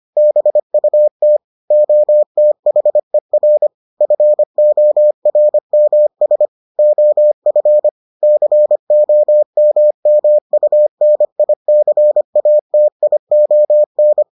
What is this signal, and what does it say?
Morse code, heard as a keyed tone, about 25 wpm, 600 Hz. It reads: BUT OTHER FORMS OF COMMUNICATION